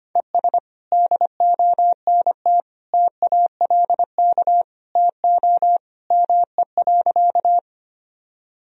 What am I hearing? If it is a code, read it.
Morse code, heard as a keyed tone, about 25 wpm, 700 Hz. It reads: ES DONT TALK TO ME.
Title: Sentences from Top 400 Words and Abbreviations 25wpm